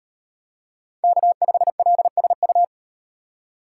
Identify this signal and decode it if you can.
Morse code, heard as a keyed tone, about 36 wpm, 700 Hz. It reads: K5LSU